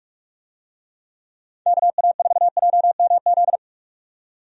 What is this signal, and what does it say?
Morse code, heard as a keyed tone, about 41 wpm, 700 Hz. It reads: KA4JMZ
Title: USA Call Signs - 6 Characters - 45wpm